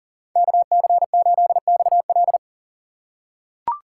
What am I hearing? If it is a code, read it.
Morse code, heard as a keyed tone, about 40 wpm, 700 Hz. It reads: KC8XL
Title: Code Groups - 5 Letter:Numbers - 40wpm